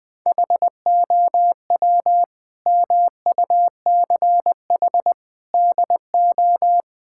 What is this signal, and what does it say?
Morse code, heard as a keyed tone, about 20 wpm, 700 Hz. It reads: HOW MUCH DO
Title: Sentences from Top 1000 Words - Review Part 6 - 20wpm